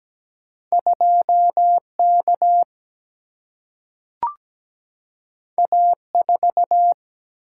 Morse code, heard as a keyed tone, about 17 wpm, 700 Hz. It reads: 2 K A 4
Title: Single-Letter Single-Number 17wpm